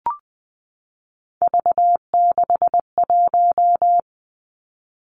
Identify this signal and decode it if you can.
Morse code, heard as a keyed tone, about 20 wpm, 700 Hz. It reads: V61
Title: Three Challenging Letter-Number Combinations 20wpm